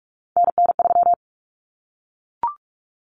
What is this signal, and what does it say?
Morse code, heard as a keyed tone, about 44 wpm, 700 Hz. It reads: NN3
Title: Common Cut Numbers - 3-Digit 45wpm